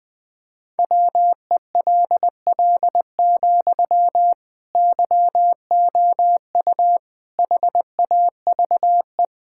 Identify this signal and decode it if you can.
Morse code, heard as a keyed tone, about 21 wpm, 700 Hz. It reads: WELL, YOU HAVE
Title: Sentences from Top 1000 Words - Review Part 6 - 20wpm